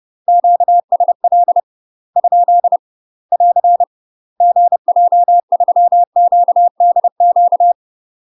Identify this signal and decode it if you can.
Morse code, heard as a keyed tone, about 30 wpm, 700 Hz. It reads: QSL ? <AR> GJ3QDQ
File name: UK MASTER QSOs TRAINING / 30WPM / UK MASTER QSOs 0008